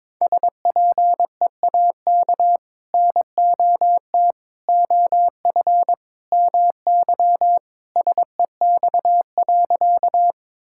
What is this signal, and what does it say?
Morse code, heard as a keyed tone, about 22 wpm, 700 Hz. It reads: SPEAK NOT OF MY SEX.
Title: Sentences from Top 1600 Words 22wpm